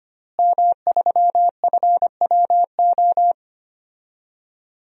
Morse code, heard as a keyed tone, about 25 wpm, 700 Hz. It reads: M3FWO
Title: DXCC-100 Call Signs - 5 Characters - Repeated 3x - 25wpm